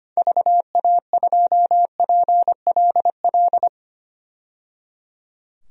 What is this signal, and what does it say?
Morse code, heard as a keyed tone, about 25 wpm, 700 Hz. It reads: VA2PLL
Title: DXCC-100 Call Signs - 6 Characters - 25wpm